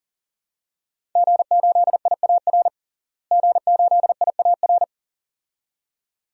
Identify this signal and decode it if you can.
Morse code, heard as a keyed tone, about 38 wpm, 700 Hz. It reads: G8IAR G8IAR
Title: DXCC-100 Call Signs - 5 Characters - Repeated 2x - 40wpm